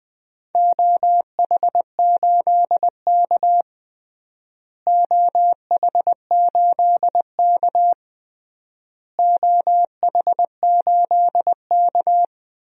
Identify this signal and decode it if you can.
Morse code, heard as a keyed tone, about 20 wpm, 700 Hz. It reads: OH8K OH8K OH8K
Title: DXCC-100 Call Signs - 4 Characters - Repeated 3x - 20wpm